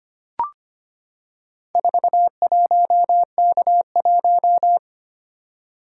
Morse code, heard as a keyed tone, about 25 wpm, 700 Hz. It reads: 41K1
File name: practice / 25wpm / Four Letter-Number Combinations - 25wpm